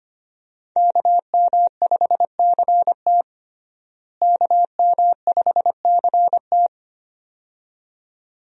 Morse code, heard as a keyed tone, about 25 wpm, 700 Hz. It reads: KM5CT KM5CT
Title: USA Call Signs - 5 Characters - Repeated 2x - 25wpm